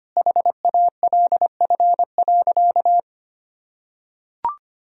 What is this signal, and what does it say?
Morse code, heard as a keyed tone, about 25 wpm, 700 Hz. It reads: HALF.
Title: Sentences from Top 400 Words and Abbreviations 25wpm